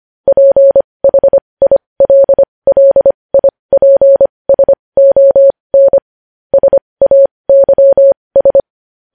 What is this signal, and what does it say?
Morse code, heard as a keyed tone, about 25 wpm, spaced as for 20 wpm, 550 Hz. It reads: PHILLIPSON SAYS
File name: podcast / News headlines-1x-25x20